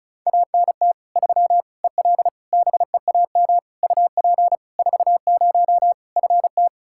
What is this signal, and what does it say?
Morse code, heard as a keyed tone, about 36 wpm, 700 Hz. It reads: ANT 3 EL BEAM UP 40 FT
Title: K1USN SST QSO Practice 35wpm